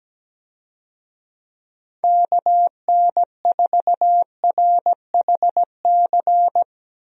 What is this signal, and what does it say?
Morse code, heard as a keyed tone, about 17 wpm, 700 Hz. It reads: KN4RHC
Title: USA Call Signs - 6 Characters - Repeated 3x - 17wpm